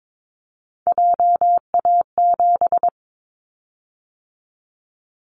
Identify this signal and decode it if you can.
Morse code, heard as a keyed tone, about 22 wpm, 700 Hz. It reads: JA7
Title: Callsign Prefixes for Top 50 DXCC - 22wpm